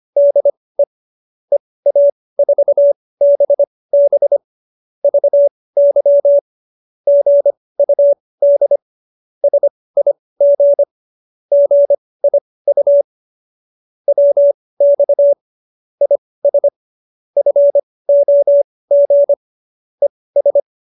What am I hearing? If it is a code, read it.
Morse code, heard as a keyed tone, about 25 wpm, spaced as for 18 wpm, 570 Hz. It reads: DE EA4BB VY GUD SIG GIU WX IS FOG ES